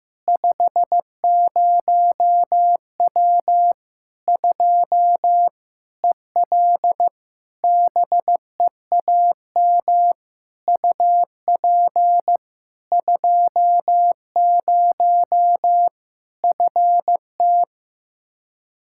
Morse code, heard as a keyed tone, about 15 wpm, 700 Hz. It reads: 50W 2 EL BEAM UP 20 FT
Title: Practice QSO - Exchange 2 - Info - 15wpm